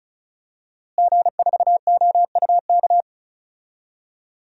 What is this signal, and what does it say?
Morse code, heard as a keyed tone, about 36 wpm, 700 Hz. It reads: G4OUK
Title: DXCC-100 Call Signs - 5 Characters - Repeated 3x - 35wpm